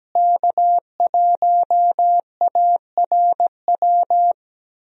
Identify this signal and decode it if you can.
Morse code, heard as a keyed tone, about 17 wpm, 700 Hz. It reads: K1ARW
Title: USA Call Signs - 5 Characters - Repeated 3x - 17wpm